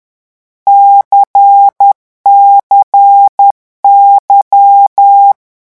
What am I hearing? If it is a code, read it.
Morse code, heard as a keyed tone, about 10 wpm, 800 Hz. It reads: CCY
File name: morse mp3 / lecon2-vitesse 10